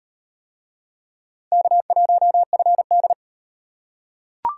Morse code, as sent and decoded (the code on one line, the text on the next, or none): -.- .---- ..-. -..
K1FD